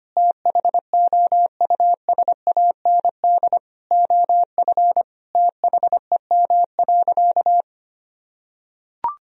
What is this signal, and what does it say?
Morse code, heard as a keyed tone, about 25 wpm, 700 Hz. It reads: THOUSAND OF THEM.
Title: Sentences from Top 1000 Words - Review Part 3 - 25wpm